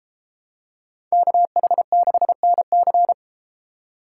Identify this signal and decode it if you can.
Morse code, heard as a keyed tone, about 31 wpm, 700 Hz. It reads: KH6NC